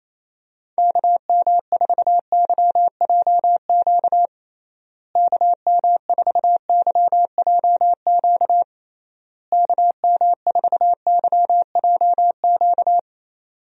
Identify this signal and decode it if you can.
Morse code, heard as a keyed tone, about 28 wpm, 700 Hz. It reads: KM4YJQ KM4YJQ KM4YJQ
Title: USA Call Signs - 6 Characters - Repeated 3x - 28wpm